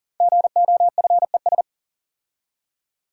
Morse code, heard as a keyed tone, about 38 wpm, 700 Hz. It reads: GOFES
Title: Code Groups - 5 Letters - 40wpm